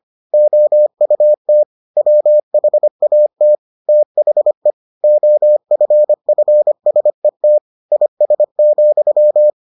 Morse code, heard as a keyed tone, about 25 wpm, 600 Hz. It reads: OUT WHAT THE OFFSET IS,